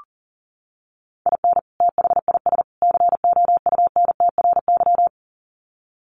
Morse code, heard as a keyed tone, about 40 wpm, 700 Hz. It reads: IN THIS COUNTRY
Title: Sets of 3 Words 40wpm